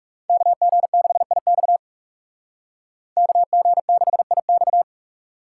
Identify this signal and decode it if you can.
Morse code, heard as a keyed tone, about 36 wpm, 700 Hz. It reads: KG6IX KG6IX
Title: Speed Racing - USA 2x2 Call Signs 30wpm